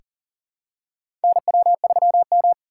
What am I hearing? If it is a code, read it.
Morse code, heard as a keyed tone, about 40 wpm, 700 Hz. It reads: NW3M
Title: USA Call Signs - 4 Characters - 2x Character Spacing - 40wpm